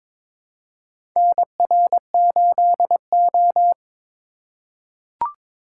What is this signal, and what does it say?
Morse code, heard as a keyed tone, about 22 wpm, 700 Hz. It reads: NR8O
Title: USA Call Signs - 4 Characters - 4x Character Spacing - 22wpm